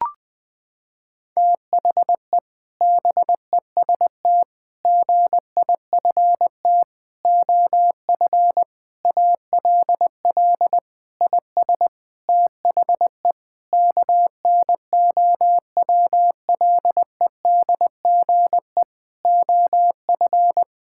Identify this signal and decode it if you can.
Morse code, heard as a keyed tone, about 21 wpm, 700 Hz. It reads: THE BEST GIFT OF ALL IS THE KNOWLEDGE OF
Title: Sentences from Top 1000 Words - Review Part 6 - 20wpm